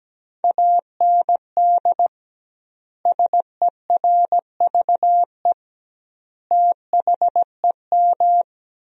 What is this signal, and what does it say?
Morse code, heard as a keyed tone, about 17 wpm, 700 Hz. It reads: AND SERVE THEM
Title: 2x Word-Spacing - Sentences from Top 400 Words - 17wpm